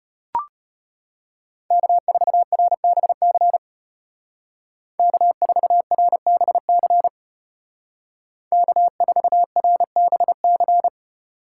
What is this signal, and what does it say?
Morse code, heard as a keyed tone, about 32 wpm, 700 Hz. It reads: K4RBC K4RBC K4RBC